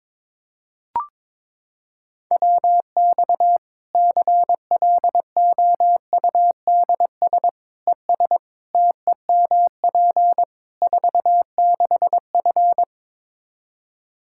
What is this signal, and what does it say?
Morse code, heard as a keyed tone, about 22 wpm, 700 Hz. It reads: WX CLOUDS ES TEMP 46F
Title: Practice QSO - Exchange 2 - Info - 22wpm